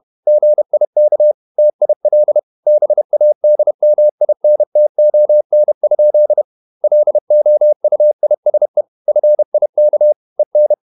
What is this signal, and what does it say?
Morse code, heard as a keyed tone, about 32 wpm, 600 Hz. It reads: GIK TIL BADMINTON? LOUISE FIK EN
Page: Ferie 30 wpm